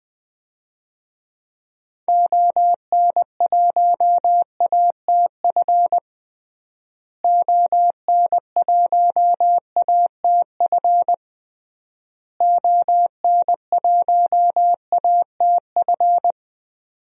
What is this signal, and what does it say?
Morse code, heard as a keyed tone, about 21 wpm, 700 Hz. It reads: ON1ATF ON1ATF ON1ATF
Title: DXCC-100 Call Signs - 6 Characters - Repeated 3x - 20wpm